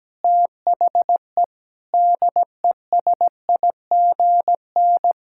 Morse code, heard as a keyed tone, about 17 wpm, 700 Hz. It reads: THE DESIGN
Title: Sentences from Top 1000 Words - Review Part 5 - 17wpm